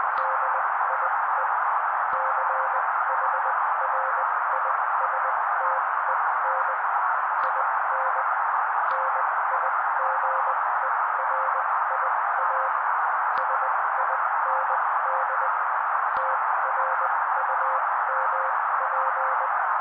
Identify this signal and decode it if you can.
Morse code, heard as a keyed tone, about 20 wpm, spaced as for 16 wpm, 530 Hz. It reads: DIE CHRISTEN IN NIGERIA SIND TRUMP
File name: daily / 16wpm noise